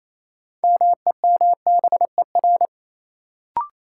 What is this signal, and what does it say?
Morse code, heard as a keyed tone, about 28 wpm, 700 Hz. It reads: MEMBER
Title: Sets of 3 Words - New Word Drill - 28wpm